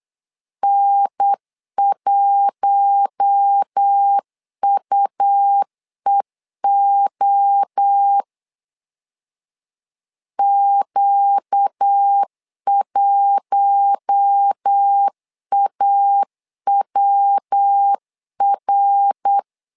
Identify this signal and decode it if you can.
Morse code, heard as a keyed tone, about 8 wpm, 800 Hz. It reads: N1UEO Q1AWR
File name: LEZIONI / 4 Q1 8wpm 3035